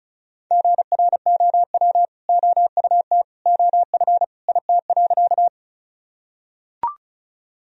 Morse code, heard as a keyed tone, about 36 wpm, 700 Hz. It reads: GROW OUT OF IT.
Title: Sentences from Top 1000 Words - Review Part 1 - 35wpm